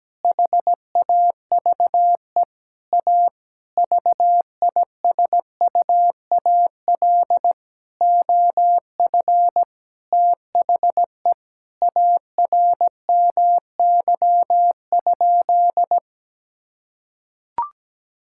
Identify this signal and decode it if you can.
Morse code, heard as a keyed tone, about 17 wpm, 700 Hz. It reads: HAVE A VISUAL OF THE ARMY?
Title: Sentences from Top 1000 Words - Review Part 6 - 17wpm